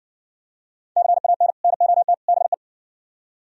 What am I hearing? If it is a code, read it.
Morse code, heard as a keyed tone, about 59 wpm, 700 Hz. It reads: CAN NOT BE